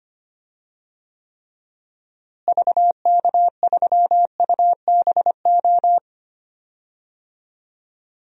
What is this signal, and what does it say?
Morse code, heard as a keyed tone, about 25 wpm, 700 Hz. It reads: VK3UBO
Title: DXCC-100 Call Signs - 6 Characters - Repeated 2x - 25wpm